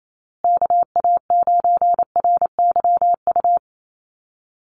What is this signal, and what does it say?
Morse code, heard as a keyed tone, about 28 wpm, 700 Hz. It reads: KA9RYU
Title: USA Call Signs Repeated 2x 28wpm